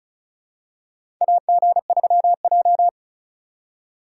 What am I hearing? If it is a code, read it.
Morse code, heard as a keyed tone, about 34 wpm, 700 Hz. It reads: AG3J
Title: USA Call Signs - 4 Characters - 2x Character Spacing - 35wpm